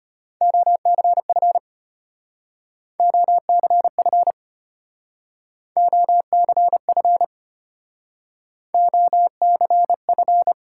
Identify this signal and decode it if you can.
Morse code, heard as a keyed tone, about 33 wpm, 700 Hz. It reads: OCF OCF OCF OCF